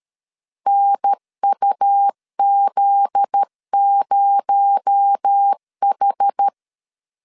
Morse code, heard as a keyed tone, about 13 wpm, 800 Hz. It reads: NUZ0H
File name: CW / DET / 1 9 1500C S A 12wpm 3800